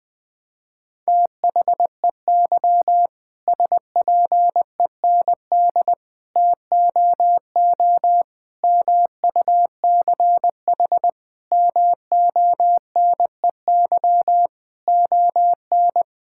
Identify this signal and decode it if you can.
Morse code, heard as a keyed tone, about 21 wpm, 700 Hz. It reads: THEY SPEND TOO MUCH MONEY ON
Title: Sentences from Top 1000 Words - Review Part 5 - 20wpm